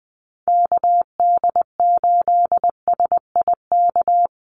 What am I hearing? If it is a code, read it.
Morse code, heard as a keyed tone, about 20 wpm, 700 Hz. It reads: KD8SIK
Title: USA Call Signs Repeated 3x 20wpm